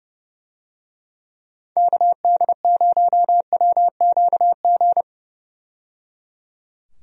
Morse code, heard as a keyed tone, about 30 wpm, 700 Hz. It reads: KD0WQG
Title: USA Call Signs - 6 Characters - 30wpm